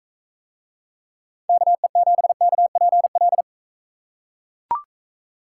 Morse code, heard as a keyed tone, about 38 wpm, 700 Hz. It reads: KE7KPL